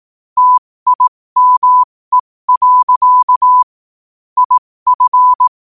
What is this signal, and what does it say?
Morse code, heard as a keyed tone, about 18 wpm, spaced as for 15 wpm, 1000 Hz. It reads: TIME. IF